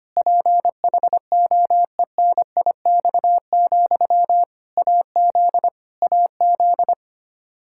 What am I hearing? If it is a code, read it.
Morse code, heard as a keyed tone, about 25 wpm, 700 Hz. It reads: PHOENIX, AZ AZ
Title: Practice QSO - Exchange 1 - Beginning - 25wpm